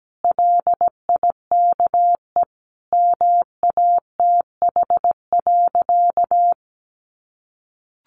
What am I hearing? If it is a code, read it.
Morse code, heard as a keyed tone, about 17 wpm, 700 Hz. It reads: LIKE MATH.
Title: Easy Phrases 17wpm